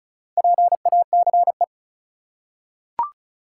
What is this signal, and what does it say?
Morse code, heard as a keyed tone, about 36 wpm, 700 Hz. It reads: PACE
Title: Four Letter Words - 35wpm